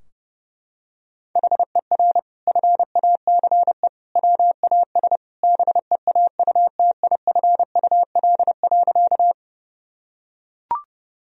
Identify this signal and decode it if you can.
Morse code, heard as a keyed tone, about 30 wpm, 700 Hz. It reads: HER FACE WAS BEAUTIFUL.
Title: Sentences from Top 1000 Words - Review Part 2 - 30wpm